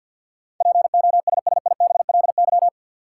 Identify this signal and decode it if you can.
Morse code, heard as a keyed tone, about 51 wpm, 700 Hz. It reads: POSSIBLY
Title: Sentences from Top 1300 Words - New Word Drill Alt - 50wpm